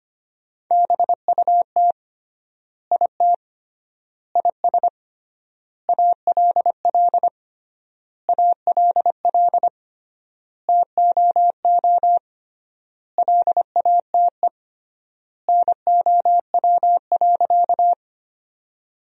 Morse code, heard as a keyed tone, about 25 wpm, 700 Hz. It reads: BUT IT IS ALL ALL TOO LATE NOW.
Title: 3x Word-Spacing - Sentences from Top 300 Words - 25wpm